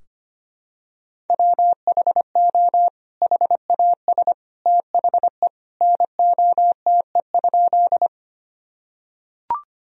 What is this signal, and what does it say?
Morse code, heard as a keyed tone, about 25 wpm, 700 Hz. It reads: WHO HAS THE NOTE?
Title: Sentences from Top 1000 Words - Review Part 2 - 25wpm